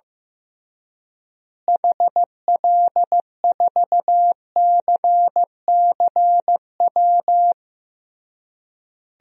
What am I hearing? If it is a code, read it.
Morse code, heard as a keyed tone, about 15 wpm, 700 Hz. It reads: HL4CCW
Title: DXCC-100 Call Signs - 6 Characters - Repeated 3x - 15wpm